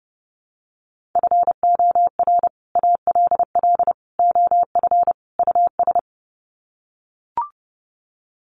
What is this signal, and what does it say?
Morse code, heard as a keyed tone, about 30 wpm, 700 Hz. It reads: FOR ALL OF US
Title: Sets of 4 Words 30wpm